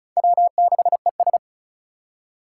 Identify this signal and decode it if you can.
Morse code, heard as a keyed tone, about 36 wpm, 700 Hz. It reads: W6ES
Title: Four Challenging Letter-Number Combinations 35wpm